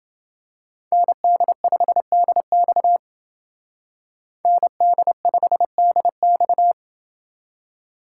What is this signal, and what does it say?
Morse code, heard as a keyed tone, about 28 wpm, 700 Hz. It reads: ND5DX ND5DX